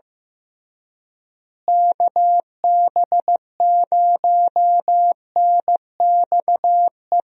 Morse code, heard as a keyed tone, about 16 wpm, 700 Hz. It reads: KB0NXE
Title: USA Call Signs - 6 Characters - Repeated 3x - 15wpm